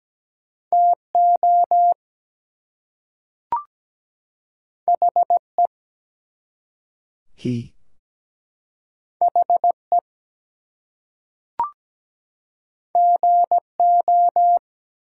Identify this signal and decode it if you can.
Morse code, heard as a keyed tone, about 17 wpm, 700 Hz. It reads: TO HE HE GO
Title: Two Letter Words - 17wpm